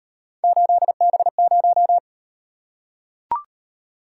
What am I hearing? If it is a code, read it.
Morse code, heard as a keyed tone, about 38 wpm, 700 Hz. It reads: 8B0